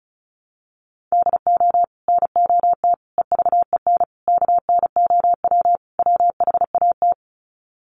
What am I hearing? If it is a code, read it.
Morse code, heard as a keyed tone, about 34 wpm, 700 Hz. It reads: DO NOT EVEN KNOW WHAT
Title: Sets of 5 Words Encore 35wpm